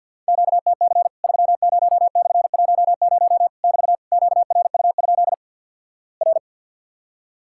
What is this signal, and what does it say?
Morse code, heard as a keyed tone, about 50 wpm, 700 Hz. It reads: QTC 30/10 <BT> QRU? R
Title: Worked All Europe DX Contest - QTC Exchange with Repeats - 50wpm